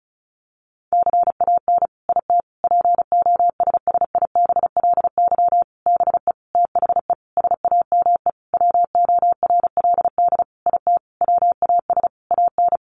Morse code, heard as a keyed tone, about 34 wpm, 700 Hz. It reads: CAN IT POSSIBLY BE THE SAME WORLD IT WAS AN
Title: Sentences from Top 1300 Words 35wpm